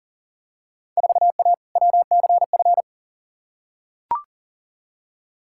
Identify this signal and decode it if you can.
Morse code, heard as a keyed tone, about 40 wpm, 700 Hz. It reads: 4A WCF
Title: Field Day Exchanges - 40wpm